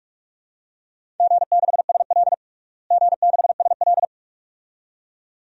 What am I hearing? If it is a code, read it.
Morse code, heard as a keyed tone, about 46 wpm, 700 Hz. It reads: G6SL G6SL
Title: DXCC-100 Call Signs - 4 Characters - Repeated 2x - 45wpm